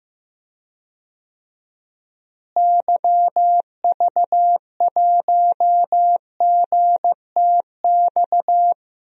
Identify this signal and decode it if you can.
Morse code, heard as a keyed tone, about 16 wpm, 700 Hz. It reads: YV1GTX
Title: DXCC-100 Call Signs - 6 Characters - Repeated 2x - 15wpm